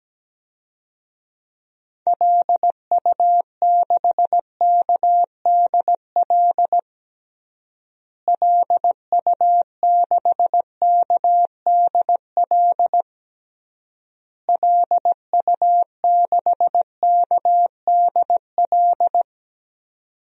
Morse code, heard as a keyed tone, about 17 wpm, 700 Hz. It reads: LU6KDL LU6KDL LU6KDL
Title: DXCC-100 Call Signs - 6 Characters - Repeated 3x - 17wpm